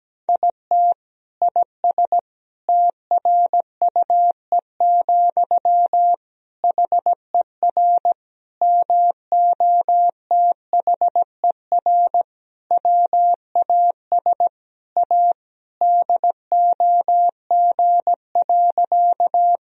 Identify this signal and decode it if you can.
Morse code, heard as a keyed tone, about 17 wpm, 700 Hz. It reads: IT IS TRUE, HER MOTHER WAS A DOG.
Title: Sentences from Top 1000 Words - Review Part 2 - 17wpm